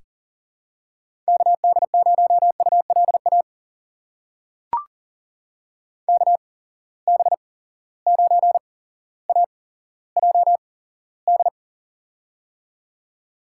Morse code, heard as a keyed tone, about 37 wpm, spaced as for 17 wpm, 700 Hz. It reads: KD0ULA KB9AJD
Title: USA Call Signs - 6 Characters - 8x Character Spacing - 40wpm